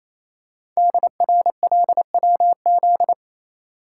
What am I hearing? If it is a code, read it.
Morse code, heard as a keyed tone, about 28 wpm, 700 Hz. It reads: DRLWZ
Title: Code Groups - 5 Letters - 28wpm